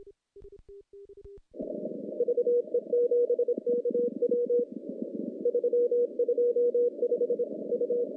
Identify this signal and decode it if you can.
Morse code, heard as a keyed tone, about 27 wpm, 480 Hz. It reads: VE7KW 325 U
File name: contests / CW-SS-k5tr- -0400